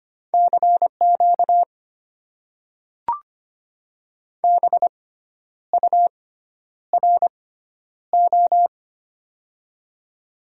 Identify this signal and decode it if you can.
Morse code, heard as a keyed tone, about 25 wpm, spaced as for 10 wpm, 700 Hz. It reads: CQ BURO
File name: Top 90 QSO Elements - 6x Character Spacing - 25wpm